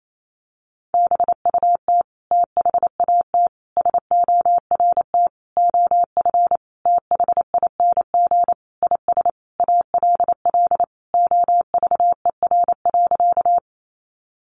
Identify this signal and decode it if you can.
Morse code, heard as a keyed tone, about 28 wpm, 700 Hz. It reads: BUT THAT SORT OF THING IS ALL OVER.
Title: Sentences from Top 1300 Words 28wpm